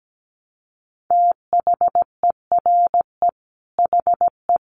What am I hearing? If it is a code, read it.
Morse code, heard as a keyed tone, about 17 wpm, 700 Hz. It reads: THERE HE
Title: Sentences from Top 100 Words 17wpm